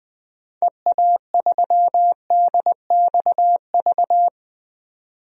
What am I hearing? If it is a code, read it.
Morse code, heard as a keyed tone, about 20 wpm, 700 Hz. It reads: EA3DXV
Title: DXCC-100 Call Signs - 6 Characters - 20wpm